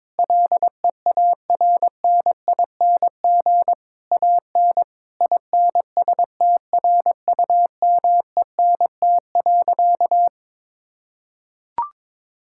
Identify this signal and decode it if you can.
Morse code, heard as a keyed tone, about 21 wpm, 700 Hz. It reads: LEARNING AN INSTRUMENT.
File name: Difficult Sentences with Hints - 22wpm